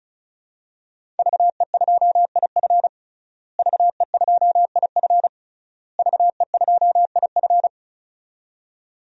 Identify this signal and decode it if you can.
Morse code, heard as a keyed tone, about 36 wpm, 700 Hz. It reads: VE2IF VE2IF VE2IF
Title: DXCC-100 Call Signs - 5 Characters - Repeated 3x - 35wpm